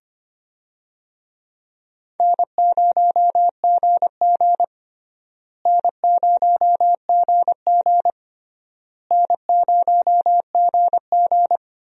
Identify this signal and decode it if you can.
Morse code, heard as a keyed tone, about 25 wpm, 700 Hz. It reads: N0GG N0GG N0GG
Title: USA Call Signs - 4 Characters - Repeated 3x - 25wpm